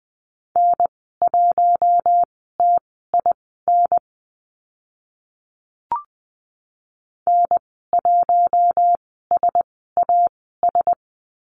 Morse code, heard as a keyed tone, about 20 wpm, spaced as for 15 wpm, 700 Hz. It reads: N1TIN N1SAS
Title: 042 Lesson 8g - Number 1 - Call Signs 20wpm